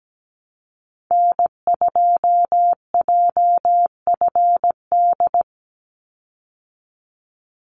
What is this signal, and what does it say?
Morse code, heard as a keyed tone, about 17 wpm, 700 Hz. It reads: N2JFD
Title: Call Signs Repeated 3x - 17wpm